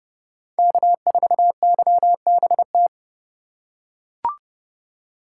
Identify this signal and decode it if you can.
Morse code, heard as a keyed tone, about 28 wpm, 700 Hz. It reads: K4YBT